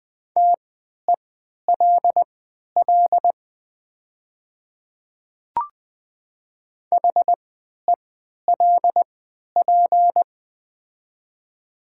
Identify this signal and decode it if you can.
Morse code, heard as a keyed tone, about 20 wpm, spaced as for 11 wpm, 700 Hz. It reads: TELL HELP
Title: 3x Character Spacing - Top 200 Words - 20wpm